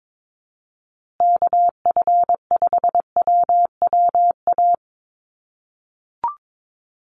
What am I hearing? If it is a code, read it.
Morse code, heard as a keyed tone, about 22 wpm, 700 Hz. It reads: KF5WWA
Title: USA Call Signs Repeated 3x 22wpm